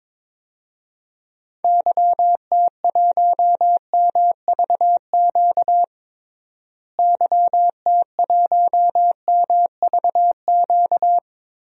Morse code, heard as a keyed tone, about 22 wpm, 700 Hz. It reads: YT1MVQ YT1MVQ
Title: DXCC-100 Call Signs - 6 Characters - Repeated 3x - 22wpm